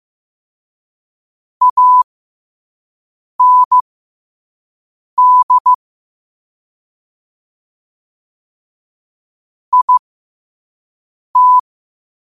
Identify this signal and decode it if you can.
Morse code, heard as a keyed tone, about 15 wpm, spaced as for 5 wpm, 1000 Hz. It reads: AND IT
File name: qotdfiles / 5wpm